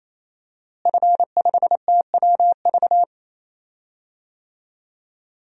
Morse code, heard as a keyed tone, about 28 wpm, 700 Hz. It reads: F5TWV
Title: DXCC-100 Call Signs - 5 Characters - Repeated 3x - 28wpm